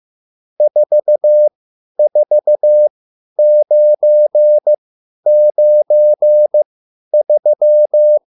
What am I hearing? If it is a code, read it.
Morse code, heard as a keyed tone, about 15 wpm, spaced as for 10 wpm, 600 Hz. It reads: 44993
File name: QSL / CW lesson